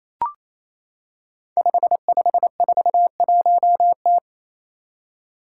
Morse code, heard as a keyed tone, about 27 wpm, 700 Hz. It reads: 5541T
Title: Five Challenging Letter-Number Combinations 28wpm